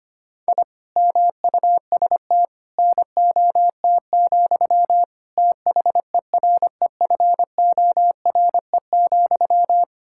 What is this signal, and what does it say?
Morse code, heard as a keyed tone, about 25 wpm, 700 Hz. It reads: I MUST NOT, THEREFORE,
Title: Sentences from Top 1000 Words - Review Part 5 - 25wpm